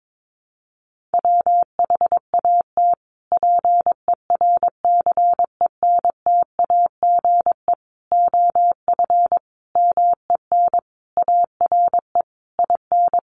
Morse code, heard as a keyed tone, about 22 wpm, 700 Hz. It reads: WHAT PERCENTAGE OF MEN ARE IN
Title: Sentences from Top 1300 Words 22wpm